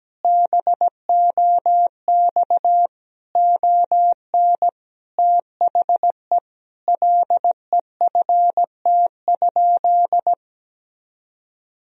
Morse code, heard as a keyed tone, about 17 wpm, 700 Hz. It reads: BOX ON THE LEFT?
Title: Sentences from Top 1000 Words - Review Part 2 - 17wpm